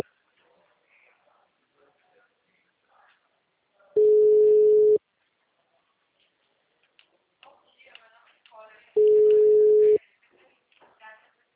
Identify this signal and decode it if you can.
Ringback tone (Europe, 425 Hz), with speech.